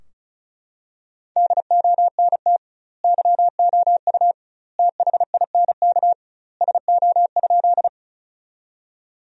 Morse code, heard as a keyed tone, about 36 wpm, 700 Hz. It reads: DONT YOU THINK SO?
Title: 2x Word-Spacing - Sentences from Top 300 Words - 35wpm